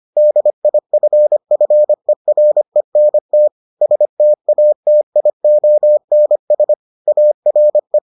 Morse code, heard as a keyed tone, about 25 wpm, 600 Hz. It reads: DIFFERENT STATIONS ARE